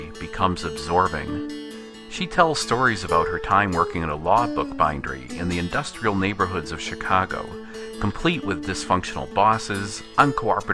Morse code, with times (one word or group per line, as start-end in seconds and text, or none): none